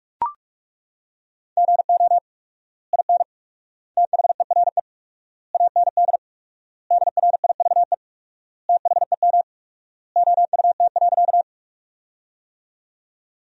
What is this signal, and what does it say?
Morse code, heard as a keyed tone, about 41 wpm, 700 Hz. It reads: GO IN THERE AND DRIVE THEM OUT.
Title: 4x Word-Spacing - Sentences from Top 500 Words - 45wpm